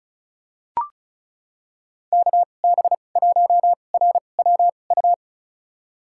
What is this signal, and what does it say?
Morse code, heard as a keyed tone, about 34 wpm, spaced as for 25 wpm, 700 Hz. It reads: KB1RWU
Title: USA Call Signs - 6 Characters - 2x Character Spacing - 35wpm